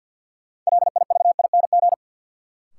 Morse code, heard as a keyed tone, about 52 wpm, 700 Hz. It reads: LIVING
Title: Sets of 4 Words - New Word Drill - 50wpm